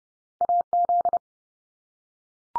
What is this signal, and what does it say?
Morse code, heard as a keyed tone, about 30 wpm, 700 Hz. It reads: AZ
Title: US States 30WPM